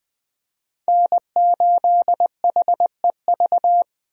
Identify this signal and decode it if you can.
Morse code, heard as a keyed tone, about 20 wpm, 700 Hz. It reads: N8HEV
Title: USA Call Signs - 5 Characters - Repeated 3x - 20wpm